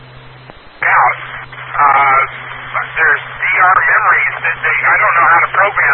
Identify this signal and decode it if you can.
Emergency Alert System (SAME) tones, heard as harsh buzzing bursts, with speech. It is heard from 1.2 to 5.0 s.